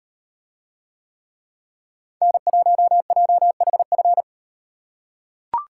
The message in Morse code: N1JHF

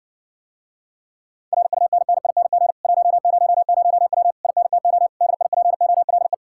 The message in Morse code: RUNNING 100W INTO DIPOLE

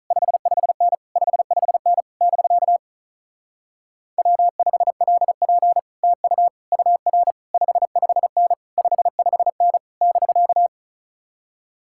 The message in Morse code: 55N 55N <BK> W5LP TU UR 55N 55N <BK>

